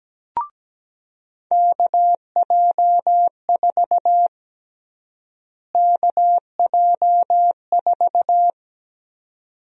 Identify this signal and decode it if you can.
Morse code, heard as a keyed tone, about 17 wpm, 700 Hz. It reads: KJ4 KJ4
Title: USA Call Sign Prefixes - Repeated 2x - 17wpm